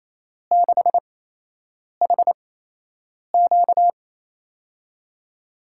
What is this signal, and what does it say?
Morse code, heard as a keyed tone, about 28 wpm, spaced as for 8 wpm, 700 Hz. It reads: 6HQ
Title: USA Call Signs - 5 Characters - 8x Character Spacing - 28wpm